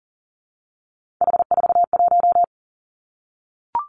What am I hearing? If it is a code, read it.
Morse code, heard as a keyed tone, about 38 wpm, 700 Hz. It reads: H41
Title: Three Challenging Letter-Number Combinations 40wpm